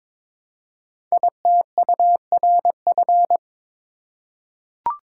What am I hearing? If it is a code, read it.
Morse code, heard as a keyed tone, about 22 wpm, 700 Hz. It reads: ITURF
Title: Code Groups - 5 Letters - 22wpm